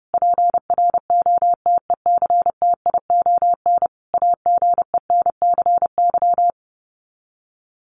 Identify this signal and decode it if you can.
Morse code, heard as a keyed tone, about 30 wpm, 700 Hz. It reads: PROTECTION AGENCY
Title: Sets of 5 Words Encore2 30wpm